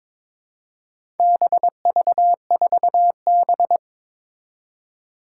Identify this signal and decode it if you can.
Morse code, heard as a keyed tone, about 22 wpm, 700 Hz. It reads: BV4B
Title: Four Challenging Letter-Number Combinations 22wpm